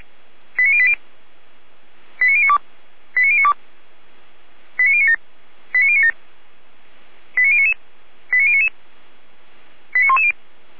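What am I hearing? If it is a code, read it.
Five-tone selective call (ZVEI), heard as quick runs of five tones, tones of about 70 ms. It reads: 89080 89092 89092 89097 89097 89099 89099 89110
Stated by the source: ZVEI five-tone selective call received with an SDR.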